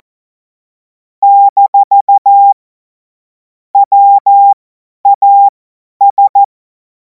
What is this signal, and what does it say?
Morse code, heard as a keyed tone, about 14 wpm, spaced as for 10 wpm, 800 Hz. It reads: - WAS
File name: Morsekurs / Witze / Witz 21. April 2026 50 F70